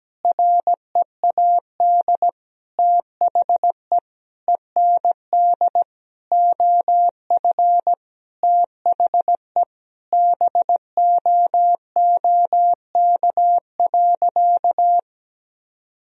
Morse code, heard as a keyed tone, about 17 wpm, 700 Hz. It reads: READ THE END OF THE BOOK.
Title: Sentences from Top 1000 Words - Review Part 1 - 17wpm